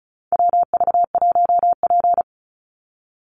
Morse code, heard as a keyed tone, about 34 wpm, 700 Hz. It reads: WV1P
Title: USA Call Signs 35wpm